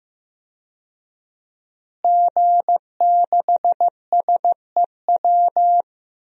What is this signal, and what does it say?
Morse code, heard as a keyed tone, about 15 wpm, 700 Hz. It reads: G6SEW
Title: DXCC-100 Call Signs - 5 Characters - Repeated 2x - 15wpm